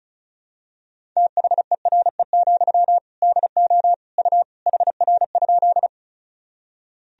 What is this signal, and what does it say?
Morse code, heard as a keyed tone, about 35 wpm, 700 Hz. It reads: THERE, DO U HR?
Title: Sentences from Top 300 Words and Abbreviations 35wpm